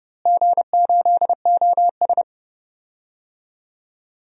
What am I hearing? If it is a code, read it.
Morse code, heard as a keyed tone, about 30 wpm, 700 Hz. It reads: G8OS
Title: DXCC-100 Call Signs - 4 Characters - Repeated 2x - 30wpm